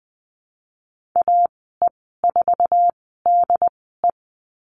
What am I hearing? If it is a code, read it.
Morse code, heard as a keyed tone, about 20 wpm, spaced as for 15 wpm, 700 Hz. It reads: AE4DE
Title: 076 Lesson 13g - Letter L - Call Signs 20wpm